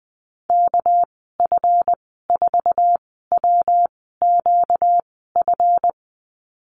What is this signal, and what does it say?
Morse code, heard as a keyed tone, about 19 wpm, spaced as for 14 wpm, 700 Hz. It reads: KF4WQF
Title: 245 Lesson 39g - Letter z - Call Signs 20wpm